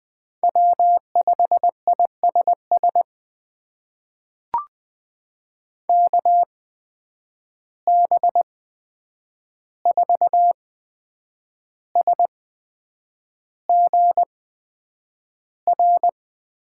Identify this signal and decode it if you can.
Morse code, heard as a keyed tone, about 20 wpm, spaced as for 8 wpm, 700 Hz. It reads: W5ISS KB4SGR
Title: USA Call Signs - 8x Character Spacing - 20wpm